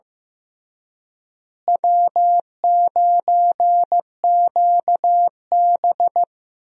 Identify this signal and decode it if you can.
Morse code, heard as a keyed tone, about 15 wpm, 700 Hz. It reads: W9QB
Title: USA Call Signs - 4 Characters - Repeated 2x - 15wpm